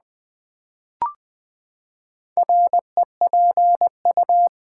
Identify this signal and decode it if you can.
Morse code, heard as a keyed tone, about 20 wpm, 700 Hz. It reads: REPU
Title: Build A Word - 4 Letters At A Time - 10-Letter Words - 20wpm